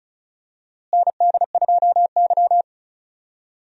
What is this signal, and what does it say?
Morse code, heard as a keyed tone, about 34 wpm, 700 Hz. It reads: ND2Y
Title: USA Call Signs - 4 Characters - Repeated 2x - 35wpm